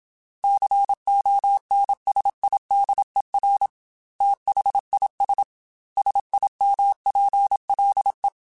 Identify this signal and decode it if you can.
Morse code, heard as a keyed tone, about 27 wpm, 800 Hz. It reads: CONSIDER THIS SIMPLE
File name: Technical Subjects / Technical Subjects-mp3s / AND Gate @25